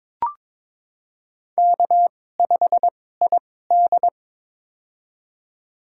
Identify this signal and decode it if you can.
Morse code, heard as a keyed tone, about 22 wpm, spaced as for 16 wpm, 700 Hz. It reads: K5ID
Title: USA Call Signs - 4 Characters - 2x Character Spacing - 22wpm